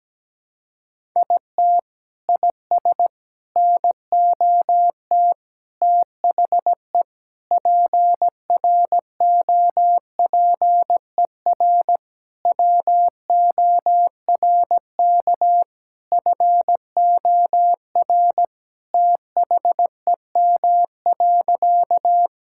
Morse code, heard as a keyed tone, about 17 wpm, 700 Hz. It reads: IT IS NOT THE PROPER WORK FOR THEM.
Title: Sentences from Top 1000 Words - Review Part 5 - 17wpm